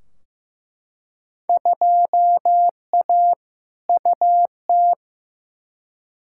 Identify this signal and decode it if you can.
Morse code, heard as a keyed tone, about 15 wpm, 700 Hz. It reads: 2A UT
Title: Field Day Exchanges - 15wpm